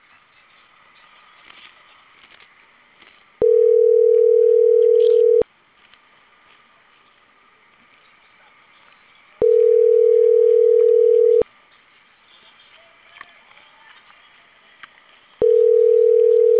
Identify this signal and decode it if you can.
Ringback tone (North America), 440 + 480 Hz.
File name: CENTRAL CAEA / out agcaea02 - .12844